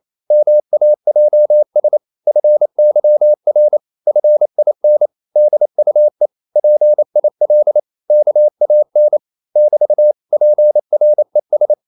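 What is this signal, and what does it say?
Morse code, heard as a keyed tone, about 28 wpm, 600 Hz. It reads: MAJS FYR FIN DUE PIL KAN <BT> PRES
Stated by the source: Danish short words sent in Morse at 28 wpm, from CW med Gnister.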